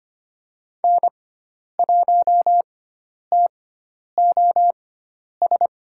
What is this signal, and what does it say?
Morse code, heard as a keyed tone, about 25 wpm, spaced as for 10 wpm, 700 Hz. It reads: N1TOS
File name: Lesson 16g - Letter c - Call Signs 25wpm